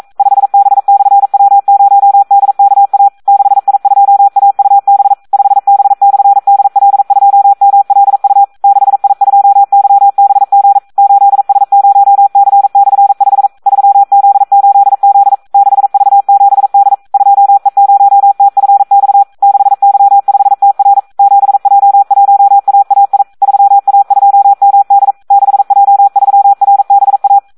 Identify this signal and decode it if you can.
Morse code, heard as a keyed tone, about 42 wpm, 800 Hz. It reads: 5B/W0DKA 6I2AUB 5B/DL2MLU 6I2YBG 8S0C/5 3Z8Z 6V7D 2E0TFX 6Y5TR 7J1AAI 3A2MD 6J3RBA
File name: Audio practica CW / SD 1 42